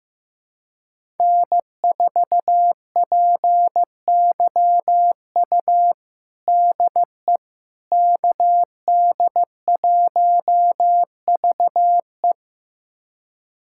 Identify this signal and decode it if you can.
Morse code, heard as a keyed tone, about 15 wpm, 700 Hz. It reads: N4PYU DE KD1VE
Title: USA Callsign DE Callsign - 5 Characters - 15wpm